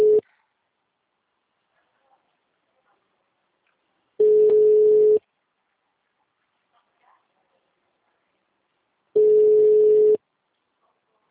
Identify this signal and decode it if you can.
Ringback tone (Europe, 425 Hz).